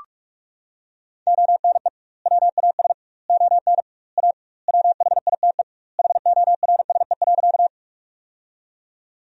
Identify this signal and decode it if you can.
Morse code, heard as a keyed tone, about 41 wpm, 700 Hz. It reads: ONE WAS ON A WHITE HORSE.
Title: 2x Word-Spacing - Sentences from Top 300 Words - 45wpm